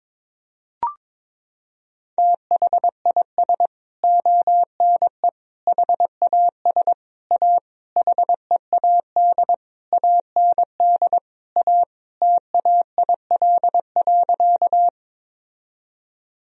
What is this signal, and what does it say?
Morse code, heard as a keyed tone, about 22 wpm, 700 Hz. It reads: THIS ONE HAS A HEAD AND A TAIL.
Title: Sentences from Top 1000 Words - Review Part 3 - 22wpm